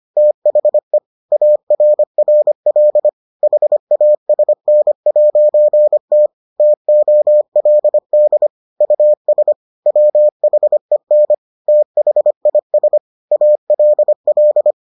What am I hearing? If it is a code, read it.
Morse code, heard as a keyed tone, about 25 wpm, 600 Hz. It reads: THE ARRL HASN'T TOLD US WHEN THIS ALL